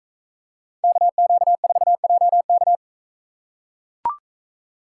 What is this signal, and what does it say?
Morse code, heard as a keyed tone, about 44 wpm, 700 Hz. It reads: KQ4JK